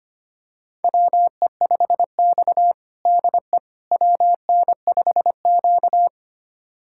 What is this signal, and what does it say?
Morse code, heard as a keyed tone, about 25 wpm, 700 Hz. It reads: WE5X DE WN5Q
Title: USA Callsign DE Callsign - 4 Characters - 25wpm